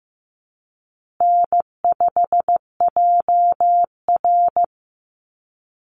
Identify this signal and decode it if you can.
Morse code, heard as a keyed tone, about 15 wpm, 700 Hz. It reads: N5JR
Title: Easy Call Signs Repeated 3x - 15wpm